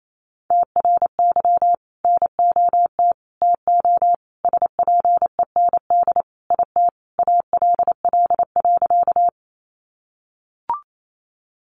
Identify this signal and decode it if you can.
Morse code, heard as a keyed tone, about 28 wpm, 700 Hz. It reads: TRY NOT TO SPEND IT ALL.
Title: Sentences from Top 800 Words 28wpm